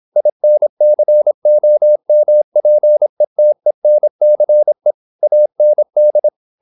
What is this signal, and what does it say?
Morse code, heard as a keyed tone, about 26 wpm, 600 Hz. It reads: INCOMPETENCE AND
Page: Never 26wpm | CW med Gnister